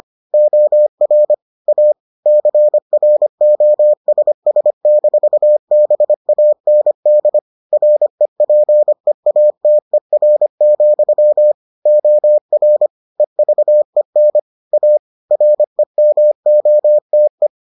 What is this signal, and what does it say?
Morse code, heard as a keyed tone, about 25 wpm, 600 Hz. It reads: OR A CROSS-BAND REPEATER, OR EVEN A REMOTE